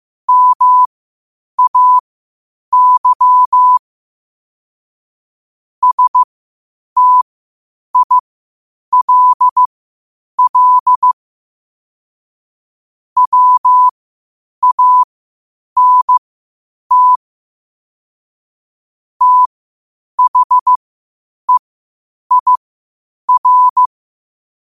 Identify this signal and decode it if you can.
Morse code, heard as a keyed tone, about 15 wpm, spaced as for 8 wpm, 1000 Hz. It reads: MAY STILL WANT THEIR
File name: qotdfiles / 8wpm